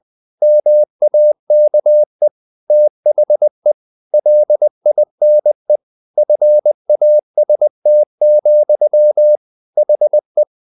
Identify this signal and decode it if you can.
Morse code, heard as a keyed tone, about 20 wpm, 600 Hz. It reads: MAKE THE LINE FAST, HE
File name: the old man and the sea part 2 20wpm